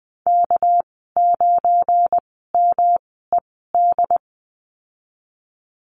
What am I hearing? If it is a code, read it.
Morse code, heard as a keyed tone, about 20 wpm, spaced as for 15 wpm, 700 Hz. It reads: K9MED
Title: 207 Lesson 33g - Letter j - Call Signs 20wpm